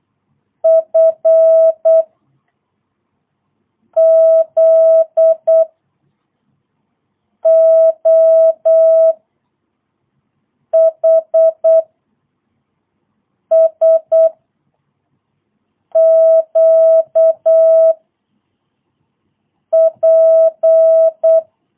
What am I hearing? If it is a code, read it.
Morse code, heard as a keyed tone, about 8 wpm, 650 Hz. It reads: F Z O H S Q P
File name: jh8cbh / 2022cw